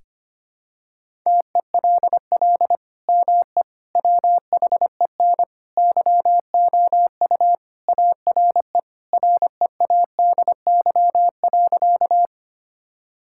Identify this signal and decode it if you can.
Morse code, heard as a keyed tone, about 25 wpm, 700 Hz. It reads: TELL ME WHEN YOU ARE READY.
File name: practice / 25wpm / Sentences with 2-Word Phrases - 25wpm